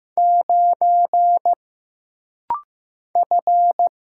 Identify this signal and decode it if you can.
Morse code, heard as a keyed tone, about 15 wpm, 700 Hz. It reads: 9 F
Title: Single Letter-Number Alt - Rapid-Fire 15wpm